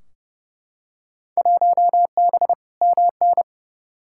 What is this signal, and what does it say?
Morse code, heard as a keyed tone, about 30 wpm, 700 Hz. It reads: 1B MN
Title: Field Day Exchanges - 30wpm